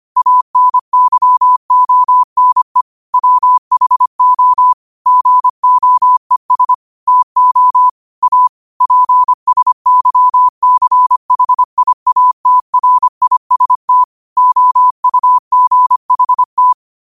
Morse code, heard as a keyed tone, about 25 wpm, 1000 Hz. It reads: ANYONE WHO GOES TO A PSYCHIATRIST OUGHT
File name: qotdfiles / 25wpm